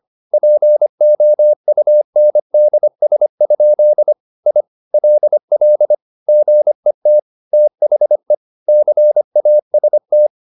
Morse code, heard as a keyed tone, about 25 wpm, 600 Hz. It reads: POUNDS? I LL GET THE CAST
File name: the old man and the sea part 1 25wpm